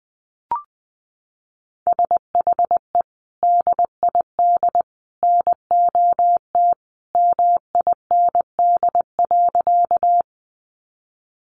Morse code, heard as a keyed tone, about 20 wpm, 700 Hz. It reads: SHE DID NOT MIND.
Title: Sentences from Top 600 Words 20wpm